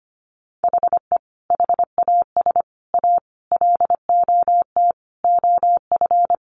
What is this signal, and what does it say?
Morse code, heard as a keyed tone, about 25 wpm, 700 Hz. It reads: HE HAS A LOT OF
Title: Additional Easy Phrases 25wpm